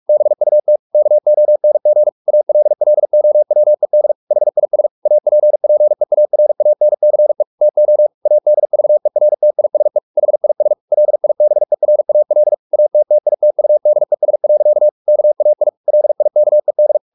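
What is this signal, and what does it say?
Morse code, heard as a keyed tone, about 44 wpm, 600 Hz. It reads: BUT KONG ALLOWED HIS APPEARANCE TO ADVERTISE HIS LIBERAL ATTITUDES. KAI LIKED
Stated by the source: Morse practice at 45 wpm, the novel Never by Ken Follett in English, from CW med Gnister.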